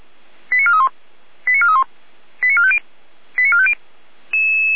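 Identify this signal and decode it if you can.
Five-tone selective call (ZVEI), heard as quick runs of five tones, tones of about 80 ms. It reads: 89431 89431 89450 89450
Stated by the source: ZVEI five-tone selective call 89324, received by SDR.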